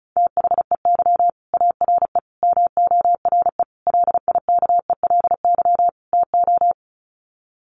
Morse code, heard as a keyed tone, about 34 wpm, 700 Hz. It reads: THEY ARE MORE LIKELY TO
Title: Sets of 5 Words Encore2 35wpm